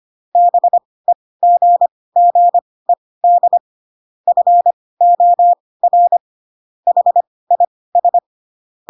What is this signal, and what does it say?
Morse code, heard as a keyed tone, about 25 wpm, spaced as for 18 wpm, 700 Hz. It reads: BEGGED FOR HIS
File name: Pinocchio ch3 12 25 ik1hge